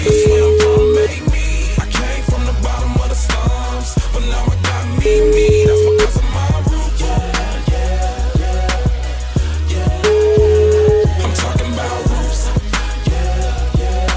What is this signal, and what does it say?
Ringback tone (Europe, 425 Hz), with noise.